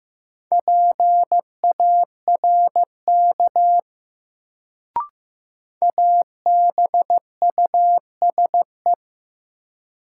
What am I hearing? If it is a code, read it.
Morse code, heard as a keyed tone, about 15 wpm, 700 Hz. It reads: PARK ABUSE
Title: Sentences from Top 1300 Words - New Word Drill - 15wpm